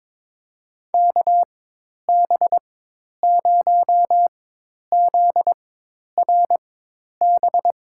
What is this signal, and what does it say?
Morse code, heard as a keyed tone, about 22 wpm, spaced as for 10 wpm, 700 Hz. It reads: KB0ZRB
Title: USA Call Signs - 6 Characters - 4x Character Spacing - 22wpm